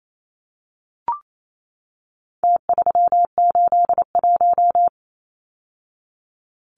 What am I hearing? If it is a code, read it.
Morse code, heard as a keyed tone, about 28 wpm, 700 Hz. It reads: T381
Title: Numbers - Contest Style - 4-Digit 28wpm